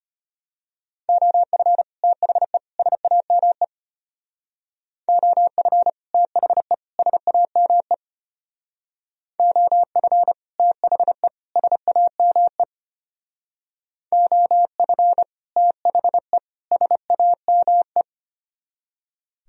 Morse code, heard as a keyed tone, about 32 wpm, 700 Hz. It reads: OF THE SAME OF THE SAME OF THE SAME OF THE SAME